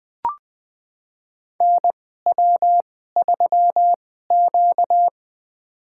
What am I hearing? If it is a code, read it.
Morse code, heard as a keyed tone, about 20 wpm, spaced as for 15 wpm, 700 Hz. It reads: NW3Q
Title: USA Call Signs - 4 Characters - 2x Character Spacing - 20wpm